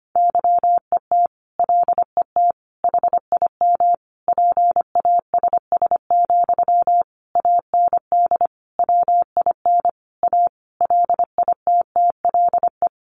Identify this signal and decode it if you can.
Morse code, heard as a keyed tone, about 25 wpm, 700 Hz. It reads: YET LET HIM PASS, AND WIN A LITTLE
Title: Sentences from Top 1500 Words 25wpm